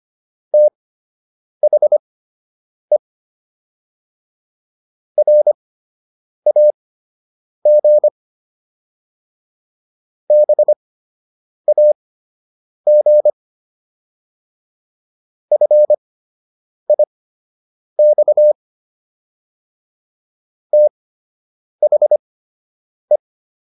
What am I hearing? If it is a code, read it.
Morse code, heard as a keyed tone, about 25 wpm, spaced as for 8 wpm, 600 Hz. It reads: THE RAG BAG FIX THE